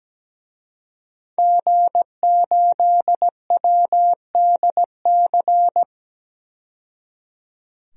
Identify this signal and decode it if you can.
Morse code, heard as a keyed tone, about 17 wpm, 700 Hz. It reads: G8WDC
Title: DXCC-100 Call Signs - 5 Characters - 17wpm